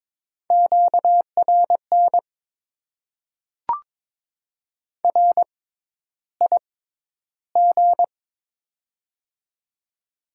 Morse code, heard as a keyed tone, about 22 wpm, spaced as for 11 wpm, 700 Hz. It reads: QRN RIG
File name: Top 90 QSO Elements - 6x Character Spacing - 22wpm